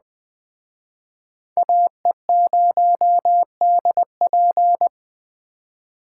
Morse code, heard as a keyed tone, about 20 wpm, 700 Hz. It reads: AE0DP
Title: Speed Racing - USA 2x2 Call Signs 17wpm